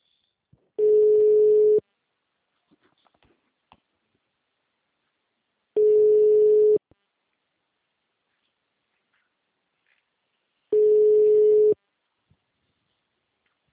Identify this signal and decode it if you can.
Ringback tone (Europe, 425 Hz).